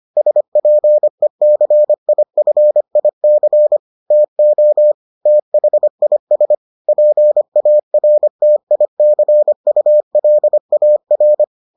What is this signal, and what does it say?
Morse code, heard as a keyed tone, about 25 wpm, 600 Hz. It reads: SPECIFIC TO THIS PARTICULAR